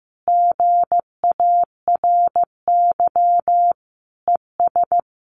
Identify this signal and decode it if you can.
Morse code, heard as a keyed tone, about 15 wpm, 700 Hz. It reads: GARY ES
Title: Common QSO Phrases Encore 15wpm